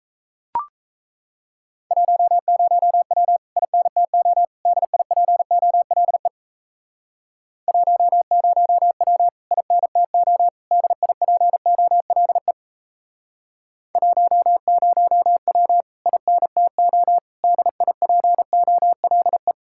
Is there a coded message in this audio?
Yes: Morse code from 1.9 to 19.5 s, heard as a keyed tone, about 37 wpm, 700 Hz. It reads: 10W INTO DIPOLE 10W INTO DIPOLE 10W INTO DIPOLE